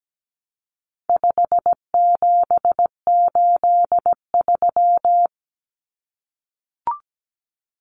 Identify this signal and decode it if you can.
Morse code, heard as a keyed tone, about 17 wpm, 700 Hz. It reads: 5783
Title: Numbers - 4-Digit 17wpm